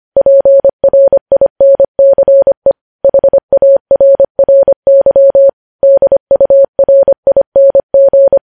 Morse code, heard as a keyed tone, about 25 wpm, 550 Hz. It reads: PRINCE HARRY DURING
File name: podcast / News headlines-1x-25x25